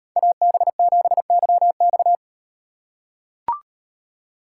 Morse code, heard as a keyed tone, about 38 wpm, 700 Hz. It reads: AB7YX